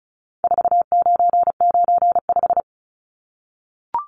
Morse code, heard as a keyed tone, about 34 wpm, 700 Hz. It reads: 4995
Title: Numbers - 4-Digit 35wpm